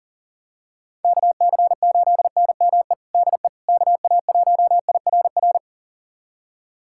Morse code, heard as a keyed tone, about 37 wpm, 700 Hz. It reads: KC8NME DE KA1IRR
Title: USA Callsign DE Callsign - 6 Characters - 40wpm